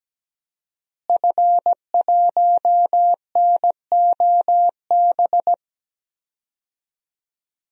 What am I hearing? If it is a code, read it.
Morse code, heard as a keyed tone, about 17 wpm, 700 Hz. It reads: F1NOB
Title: DXCC-100 Call Signs - 5 Characters - Repeated 3x - 17wpm